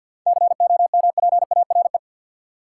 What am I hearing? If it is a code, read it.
Morse code, heard as a keyed tone, about 53 wpm, 700 Hz. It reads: COMPARE
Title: Seven Letter Words - 50wpm